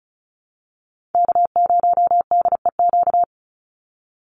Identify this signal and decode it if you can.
Morse code, heard as a keyed tone, about 34 wpm, 700 Hz. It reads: K0DEQ
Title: Call Signs Repeated 2x - 35wpm